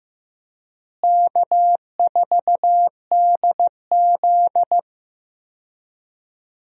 Morse code, heard as a keyed tone, about 15 wpm, 700 Hz. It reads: K4DZ
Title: USA Call Signs - 4 Characters - Repeated 2x - 15wpm